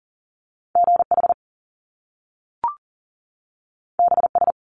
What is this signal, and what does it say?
Morse code, heard as a keyed tone, about 42 wpm, 700 Hz. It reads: GH BS
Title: Top Two Letter Combinations in Words 40wpm